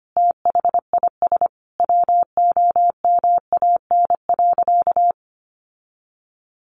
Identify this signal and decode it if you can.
Morse code, heard as a keyed tone, about 25 wpm, 700 Hz. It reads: THIS WOMAN.
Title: Sentences from Top 1100 Words 25wpm